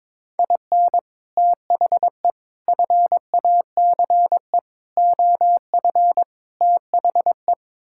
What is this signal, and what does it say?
Morse code, heard as a keyed tone, about 22 wpm, 700 Hz. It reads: IN THE FACE OF THE
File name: practice / 22wpm / Build A Phrase - 5-Word Phrases - 22wpm